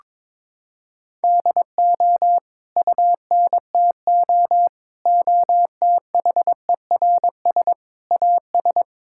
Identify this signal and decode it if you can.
Morse code, heard as a keyed tone, about 22 wpm, 700 Hz. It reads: DO UNTO OTHERS AS
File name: Sets of 9 Words - 22wpm